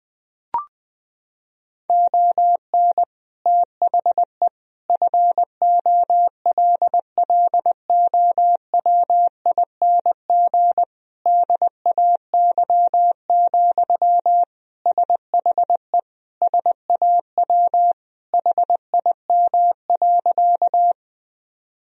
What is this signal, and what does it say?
Morse code, heard as a keyed tone, about 21 wpm, 700 Hz. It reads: ON THE FOLLOWING DAY, SHE SAW HIM.
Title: Sentences from Top 1000 Words - Review Part 4 - 20wpm